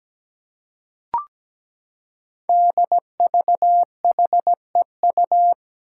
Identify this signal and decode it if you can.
Morse code, heard as a keyed tone, about 17 wpm, 700 Hz. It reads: DVHEU
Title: Five Challenging Letter-Number Combinations 17wpm